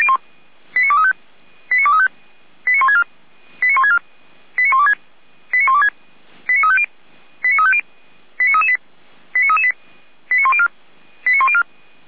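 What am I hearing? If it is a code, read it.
Five-tone selective call (ZVEI), heard as quick runs of five tones, tones of about 70 ms. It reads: 89235 89235 89164 89164 89126 89126 89350 89350 89338 89338 89104 89104
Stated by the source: ZVEI five-tone selective call 89700, received on an SDR.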